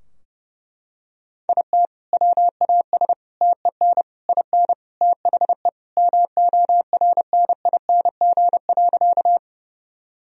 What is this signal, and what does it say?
Morse code, heard as a keyed tone, about 30 wpm, 700 Hz. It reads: IT WAS TEN IN THE MORNING.
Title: Sentences from Top 1000 Words - Review Part 2 - 30wpm